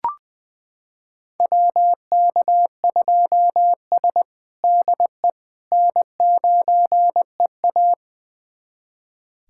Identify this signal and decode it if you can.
Morse code, heard as a keyed tone, about 20 wpm, 700 Hz. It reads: WK2S DE N9EA
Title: USA Callsign DE Callsign - 4 Characters - 20wpm